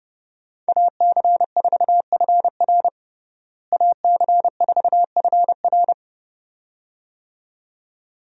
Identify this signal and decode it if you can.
Morse code, heard as a keyed tone, about 30 wpm, 700 Hz. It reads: AC4FR AC4FR
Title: USA Call Signs - 5 Characters - Repeated 2x - 30wpm